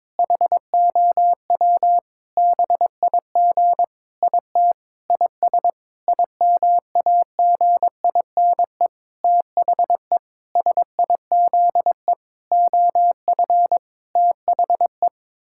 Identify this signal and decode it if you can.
Morse code, heard as a keyed tone, about 22 wpm, 700 Hz. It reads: HOW BIG IT IS IMAGINE THE SIZE OF THE
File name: Sentences with 7-Word Phrases - 22wpm